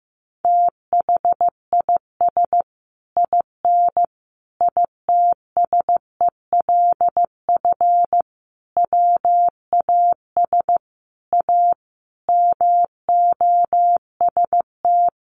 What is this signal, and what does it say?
Morse code, heard as a keyed tone, about 15 wpm, 700 Hz. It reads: THIS IN ITSELF WAS A MOST
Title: Sentences from Top 1400 Words 15wpm